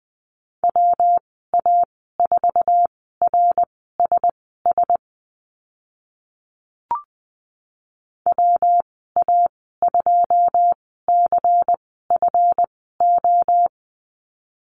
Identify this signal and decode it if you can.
Morse code, heard as a keyed tone, about 20 wpm, spaced as for 15 wpm, 700 Hz. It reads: WA4RSS WA2CFO
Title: 161 Lesson 26g - Letter g - Call Signs 20wpm